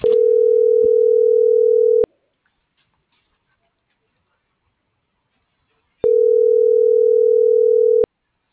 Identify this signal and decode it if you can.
Ringback tone (North America), 440 + 480 Hz.